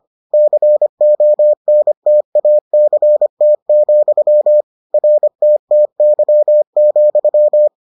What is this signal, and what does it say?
Morse code, heard as a keyed tone, about 25 wpm, 600 Hz. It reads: CONTACT, RTTY,